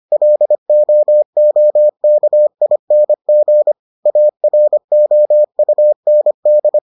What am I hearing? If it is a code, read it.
Morse code, heard as a keyed tone, about 25 wpm, 600 Hz. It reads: LOOKING AROUND